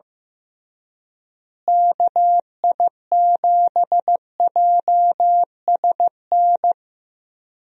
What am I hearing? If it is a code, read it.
Morse code, heard as a keyed tone, about 15 wpm, 700 Hz. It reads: KI7JSN
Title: USA Call Signs - 6 Characters - Repeated 3x - 15wpm